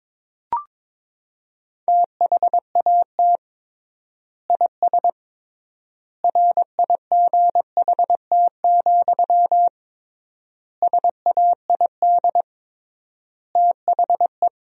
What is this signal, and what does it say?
Morse code, heard as a keyed tone, about 22 wpm, 700 Hz. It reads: THAT IS RIGHT, SAID THE
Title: 3x Word-Spacing - Sentences from Top 200 Words - 22wpm